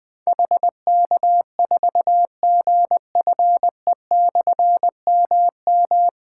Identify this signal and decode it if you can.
Morse code, heard as a keyed tone, about 20 wpm, 700 Hz. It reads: HK4GFE/MM
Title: Difficult Call Signs 20wpm